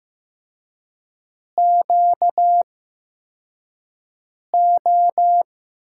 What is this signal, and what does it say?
Morse code, heard as a keyed tone, about 16 wpm, spaced as for 4 wpm, 700 Hz. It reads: QO
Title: USA Call Signs - 5 Characters - 8x Character Spacing - 15wpm